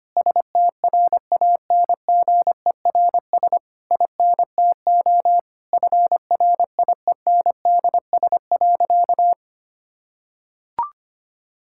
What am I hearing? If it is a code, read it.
Morse code, heard as a keyed tone, about 25 wpm, 700 Hz. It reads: STRANGERS INTO FRIENDS.
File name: practice / 25wpm / Difficult Sentences with Hints - 25wpm